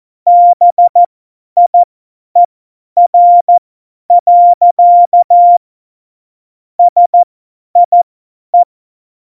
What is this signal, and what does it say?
Morse code, heard as a keyed tone, about 14 wpm, spaced as for 10 wpm, 700 Hz. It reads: BIER. SIE
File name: Morsekurs / Witze / Witz 11. Februar 2020, 50 F70